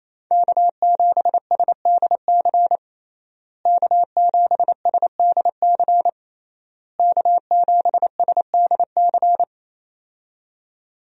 Morse code, heard as a keyed tone, about 28 wpm, 700 Hz. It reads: K7SDC K7SDC K7SDC
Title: USA Call Signs - 5 Characters - Repeated 3x - 28wpm